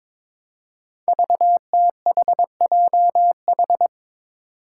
Morse code, heard as a keyed tone, about 22 wpm, 700 Hz. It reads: VTHJH
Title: Five Challenging Letter-Number Combinations 22wpm